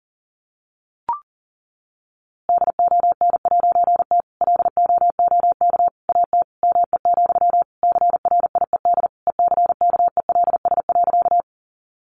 Morse code, heard as a keyed tone, about 40 wpm, 700 Hz. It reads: DON'T LOOK AT ME, CRIED ECKELS.
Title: A Sound Of Thunder 40wpm